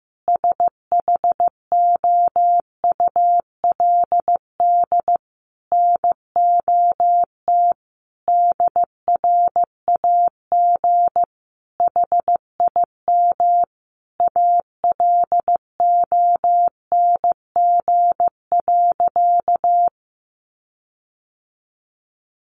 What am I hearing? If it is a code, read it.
Morse code, heard as a keyed tone, about 15 wpm, 700 Hz. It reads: SHOULD NOT DRAG HIM ALONG.
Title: Sentences from Top 1500 Words 15wpm